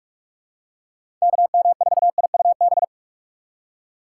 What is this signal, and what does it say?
Morse code, heard as a keyed tone, about 46 wpm, 700 Hz. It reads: KM4IUB
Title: USA Call Signs - 6 Characters - 3x Character Spacing - 45wpm